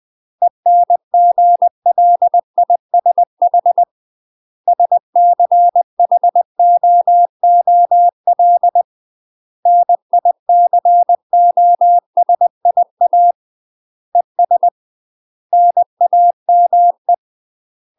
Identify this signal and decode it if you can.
Morse code, heard as a keyed tone, about 20 wpm, 700 Hz. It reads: ENGLISH SCHOOL NICOSIA ES NAME
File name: UK MASTER QSOs TRAINING / 20WPM / UK MASTER QSOs 0037